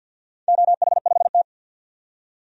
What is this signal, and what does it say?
Morse code, heard as a keyed tone, about 52 wpm, 700 Hz. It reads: OH5T
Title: DXCC-100 Call Signs - 4 Characters - 50wpm